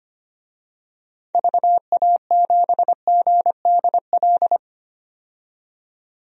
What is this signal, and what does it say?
Morse code, heard as a keyed tone, about 25 wpm, 690 Hz. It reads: VA7GDL
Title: DXCC-100 Call Signs - 6 Characters - 25wpm